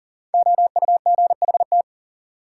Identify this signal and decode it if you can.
Morse code, heard as a keyed tone, about 40 wpm, 700 Hz. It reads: OUGHT
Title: Sets of 5 Words Encore - New Word Drill - 40wpm